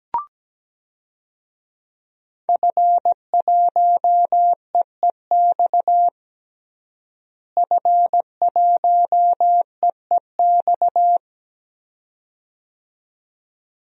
Morse code, heard as a keyed tone, about 17 wpm, 700 Hz. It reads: F1EEX F1EEX
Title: DXCC-100 Call Signs - 5 Characters - Repeated 2x - 17wpm